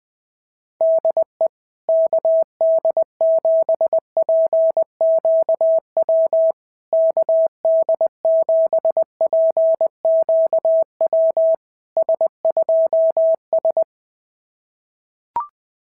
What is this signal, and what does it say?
Morse code, heard as a keyed tone, about 21 wpm, 660 Hz. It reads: DE KD7PQW KD7PQW S2S
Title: SOTA QSO Practice 20wpm